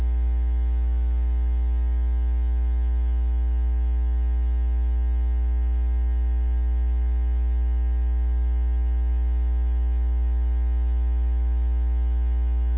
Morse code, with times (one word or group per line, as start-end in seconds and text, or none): none